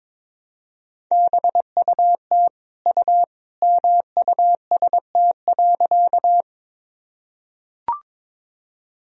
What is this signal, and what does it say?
Morse code, heard as a keyed tone, about 22 wpm, 700 Hz. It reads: BUT U MUST.
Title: Sentences from Top 200 Words and Abbreviations 22wpm